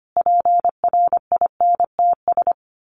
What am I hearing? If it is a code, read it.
Morse code, heard as a keyed tone, about 25 wpm, 700 Hz. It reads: PRINTS
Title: A Sound Of Thunder 25wpm